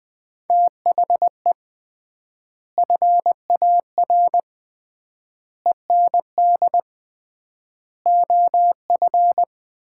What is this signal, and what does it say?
Morse code, heard as a keyed tone, about 21 wpm, 690 Hz. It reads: THE FAR END OF
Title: 3x Word-Spacing - Sets of 5 Words - 20wpm